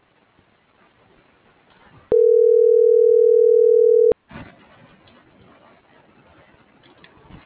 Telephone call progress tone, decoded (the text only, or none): ringback tone